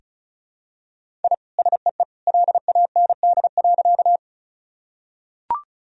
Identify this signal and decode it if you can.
Morse code, heard as a keyed tone, about 36 wpm, 700 Hz. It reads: I SEE LAND.
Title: Sentences from Top 1000 Words - Review Part 1 - 35wpm